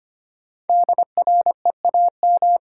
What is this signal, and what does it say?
Morse code, heard as a keyed tone, about 25 wpm, 700 Hz. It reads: DREAM
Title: Sentences from Top 1500 Words - New Word Drill - 25wpm